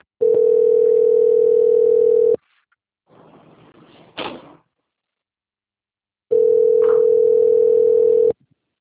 Ringback tone (North America), 440 + 480 Hz.